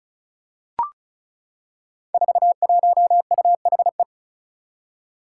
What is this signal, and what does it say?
Morse code, heard as a keyed tone, about 36 wpm, 700 Hz. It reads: 41UHE
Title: Five Challenging Letter-Number Combinations 35wpm